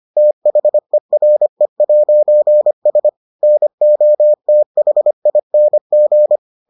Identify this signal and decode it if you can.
Morse code, heard as a keyed tone, about 25 wpm, 600 Hz. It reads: THERE'S NOTHING